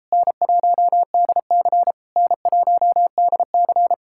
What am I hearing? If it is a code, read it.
Morse code, heard as a keyed tone, about 37 wpm, 700 Hz. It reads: N1DC N1DC